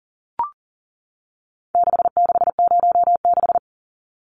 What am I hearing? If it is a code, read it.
Morse code, heard as a keyed tone, about 40 wpm, 700 Hz. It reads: 6606
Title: Numbers - 4-Digit 40wpm